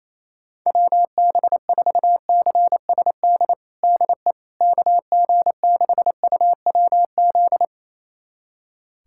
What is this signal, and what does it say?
Morse code, heard as a keyed tone, about 27 wpm, 700 Hz. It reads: WB4CSD DE KG6UWZ
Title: USA Callsign DE Callsign 28wpm